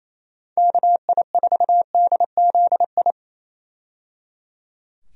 Morse code, heard as a keyed tone, about 27 wpm, 700 Hz. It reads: KI4DZI
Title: USA Call Signs - 6 Characters - 28wpm